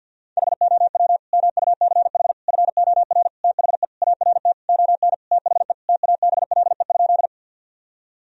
Morse code, heard as a keyed tone, about 50 wpm, 700 Hz. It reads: HOW MUCH FOR THE ART ON THE TABLE?
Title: Sentences from Top 1000 Words - Review Part 3 - 50wpm